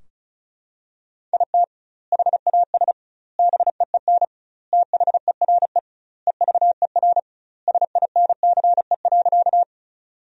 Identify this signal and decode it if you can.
Morse code, heard as a keyed tone, about 34 wpm, 700 Hz. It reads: IT HAS BEEN THERE EVER SINCE.
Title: 2x Word-Spacing - Sentences from Top 400 Words - 35wpm